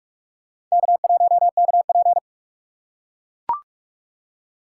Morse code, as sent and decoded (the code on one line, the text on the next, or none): -.- .---- -.- .--.
K1KP